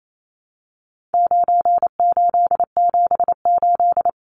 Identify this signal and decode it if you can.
Morse code, heard as a keyed tone, about 28 wpm, 700 Hz. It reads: 9878
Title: Numbers - 4-Digit 28wpm